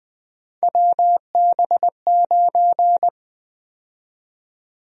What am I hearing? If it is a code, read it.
Morse code, heard as a keyed tone, about 21 wpm, 700 Hz. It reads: WB9
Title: USA Call Sign Prefixes - 20wpm